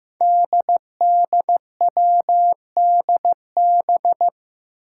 Morse code, heard as a keyed tone, about 16 wpm, 700 Hz. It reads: DDWDB
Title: Five Challenging Letter-Number Combinations 15wpm